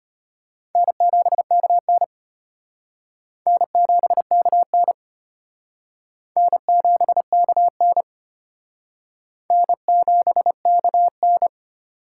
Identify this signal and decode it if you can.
Morse code, heard as a keyed tone, about 33 wpm, 700 Hz. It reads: N7KN N7KN N7KN N7KN